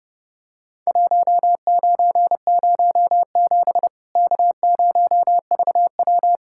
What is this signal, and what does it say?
Morse code, heard as a keyed tone, about 32 wpm, 700 Hz. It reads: 1907 K0VW
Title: Worked All Europe DX Contest - QTC Exchange - 30wpm